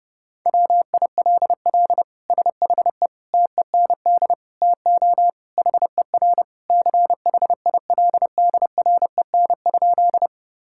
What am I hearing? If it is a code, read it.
Morse code, heard as a keyed tone, about 30 wpm, 700 Hz. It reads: WILL SHE TEND TO HER CHILDREN?
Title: Sentences from Top 1000 Words - Review Part 5 - 30wpm